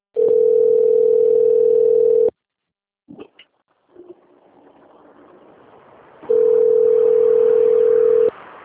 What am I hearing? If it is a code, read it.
Ringback tone (North America), 440 + 480 Hz.